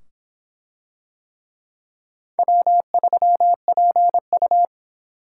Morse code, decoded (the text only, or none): W3PU